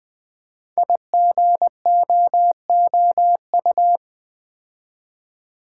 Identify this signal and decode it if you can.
Morse code, heard as a keyed tone, about 20 wpm, 700 Hz. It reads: IGOOU
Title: Code Groups - 5 Letters - 20wpm